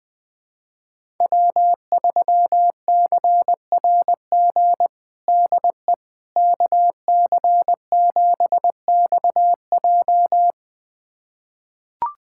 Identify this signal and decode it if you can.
Morse code, heard as a keyed tone, about 20 wpm, 700 Hz. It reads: W3CRG DE KC7XJ
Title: USA Callsign DE Callsign - 5 Characters - 20wpm